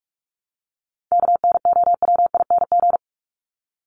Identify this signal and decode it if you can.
Morse code, heard as a keyed tone, about 44 wpm, 700 Hz. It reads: KNOWING
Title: ING Suffix - Top 100 Words with Base Words 45wpm